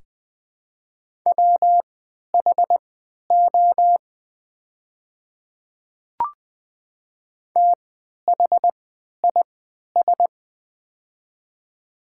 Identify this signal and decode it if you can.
Morse code, heard as a keyed tone, about 20 wpm, spaced as for 11 wpm, 700 Hz. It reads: WHO THIS
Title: 3x Character Spacing - Top 200 Words - 20wpm